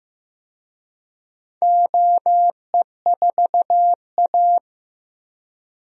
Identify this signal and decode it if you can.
Morse code, heard as a keyed tone, about 15 wpm, 700 Hz. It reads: OE4A
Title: DXCC-100 Call Signs - 4 Characters - Repeated 2x - 15wpm